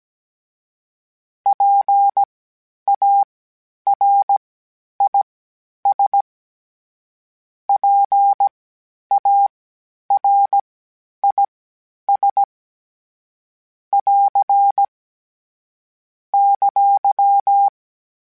Morse code, heard as a keyed tone, about 17 wpm, spaced as for 10 wpm, 800 Hz. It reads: PARIS PARIS <AR> !
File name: PARIS Calibration / PARIS mp3s / PARIS@10-@16